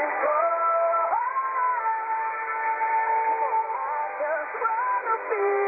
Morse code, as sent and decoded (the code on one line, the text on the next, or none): none